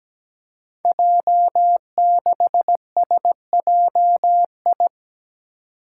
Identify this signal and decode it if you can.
Morse code, heard as a keyed tone, about 17 wpm, 700 Hz. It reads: J6SJI
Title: Five Challenging Letter-Number Combinations 17wpm